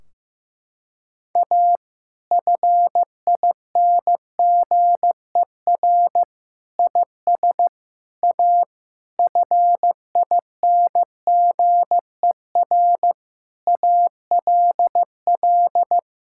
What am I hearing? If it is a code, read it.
Morse code, heard as a keyed tone, about 15 wpm, 700 Hz. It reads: A FINGER IS A FINGER ALL
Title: Sentences from Top 1600 Words 15wpm